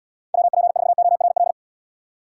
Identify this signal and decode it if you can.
Morse code, heard as a keyed tone, about 75 wpm, 700 Hz. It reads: WP4ORL